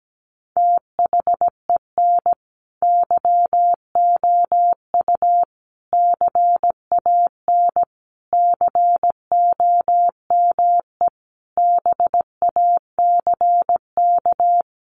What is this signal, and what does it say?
Morse code, heard as a keyed tone, about 17 wpm, 700 Hz. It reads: THEN YOU CAN COME BACK
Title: A Sound Of Thunder 17wpm